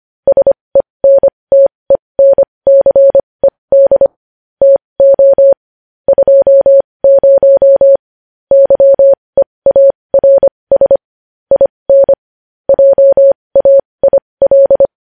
Morse code, heard as a keyed tone, about 25 wpm, spaced as for 20 wpm, 550 Hz. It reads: SENTENCED TO 20 YEARS IN JAIL
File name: podcast / News headlines-1x-25x20